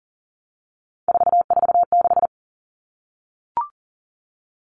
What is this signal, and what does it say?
Morse code, heard as a keyed tone, about 40 wpm, 700 Hz. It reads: 446
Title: Numbers - 3-Digit 40wpm